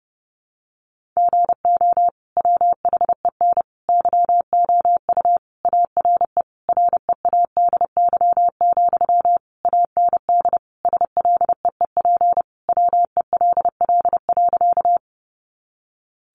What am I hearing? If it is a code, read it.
Morse code, heard as a keyed tone, about 30 wpm, 700 Hz. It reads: GO WHEN YOU ARE READY, AND SLEEP WELL.
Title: Sentences from Top 500 Words 30wpm